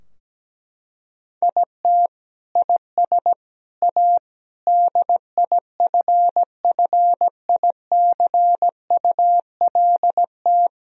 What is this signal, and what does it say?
Morse code, heard as a keyed tone, about 17 wpm, 700 Hz. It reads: IT IS A DIFFICULT
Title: Sentences from Top 1000 Words - Review Part 4 - 17wpm